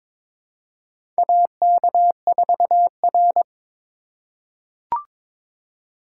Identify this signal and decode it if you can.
Morse code, heard as a keyed tone, about 22 wpm, 700 Hz. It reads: AK4R
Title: USA Call Signs - 4 Characters - 2x Character Spacing - 22wpm